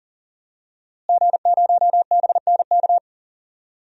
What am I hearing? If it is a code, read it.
Morse code, heard as a keyed tone, about 40 wpm, 700 Hz. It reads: G0BNK
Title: DXCC-100 Call Signs - 5 Characters - Repeated 2x - 40wpm